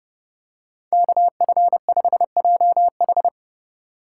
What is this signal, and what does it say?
Morse code, heard as a keyed tone, about 27 wpm, 700 Hz. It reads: KF5JH